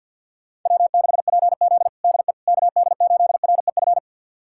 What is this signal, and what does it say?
Morse code, heard as a keyed tone, about 46 wpm, 700 Hz. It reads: W6PZ DE KD8REF
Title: USA Callsign DE Callsign 50wpm